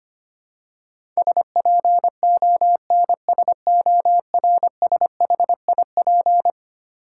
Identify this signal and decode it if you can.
Morse code, heard as a keyed tone, about 25 wpm, 700 Hz. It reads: SPONSORSHIP
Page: morse-code-ninja-practice-25wpm